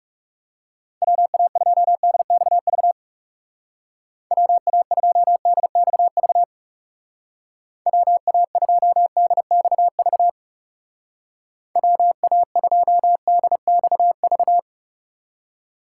Morse code, heard as a keyed tone, about 38 wpm, 700 Hz. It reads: WA2DXV WA2DXV WA2DXV WA2DXV